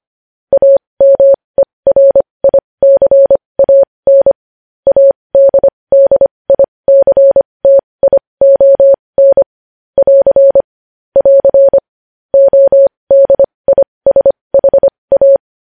Morse code, heard as a keyed tone, about 25 wpm, spaced as for 20 wpm, 550 Hz. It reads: AMERICAN ADDICTION <AR> <AR> ODISHA
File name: podcast / News headlines-1x-25x20